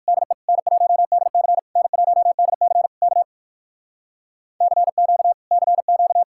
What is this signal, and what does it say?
Morse code, heard as a keyed tone, about 65 wpm, 700 Hz. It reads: DE N1DC N1DC K CQ CQ